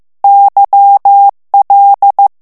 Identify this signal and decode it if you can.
Morse code, heard as a keyed tone, about 15 wpm, 800 Hz. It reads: YL
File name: ITALIEN texte clair 1 vitesse 14 mots minute